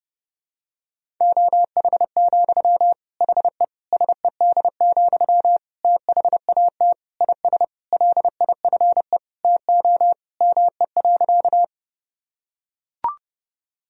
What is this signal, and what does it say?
Morse code, heard as a keyed tone, about 30 wpm, 700 Hz. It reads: OH, HE SED, THAT IS LIFE TO ME.
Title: Sentences from Top 500 Words and Abbreviations 30wpm